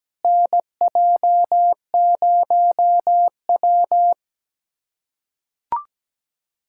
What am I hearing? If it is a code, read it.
Morse code, heard as a keyed tone, about 17 wpm, 700 Hz. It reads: NJ0W
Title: USA Callsign DE Callsign - 4 Characters - 17wpm